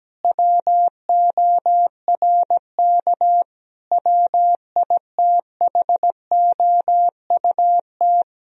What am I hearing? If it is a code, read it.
Morse code, heard as a keyed tone, about 17 wpm, 700 Hz. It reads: WORK WITHOUT
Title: Sentences from Top 1000 Words - Review Part 3 - 17wpm